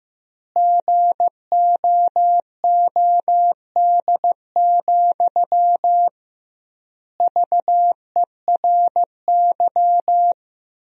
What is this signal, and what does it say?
Morse code, heard as a keyed tone, about 16 wpm, 700 Hz. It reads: GOOD, VERY
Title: 2x Word-Spacing - Sentences from Top 200 Words - 15wpm